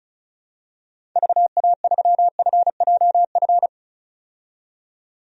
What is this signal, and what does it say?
Morse code, heard as a keyed tone, about 36 wpm, 700 Hz. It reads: VA3FJF
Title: DXCC-100 Call Signs - 6 Characters - 35wpm